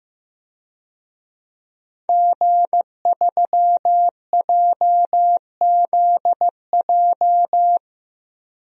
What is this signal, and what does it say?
Morse code, heard as a keyed tone, about 15 wpm, 700 Hz. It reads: G3JZJ
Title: DXCC-100 Call Signs - 5 Characters - 15wpm